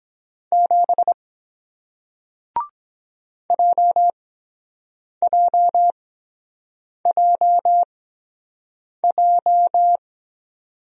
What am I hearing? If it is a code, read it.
Morse code, heard as a keyed tone, about 23 wpm, spaced as for 7 wpm, 700 Hz. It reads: 7 JJJJ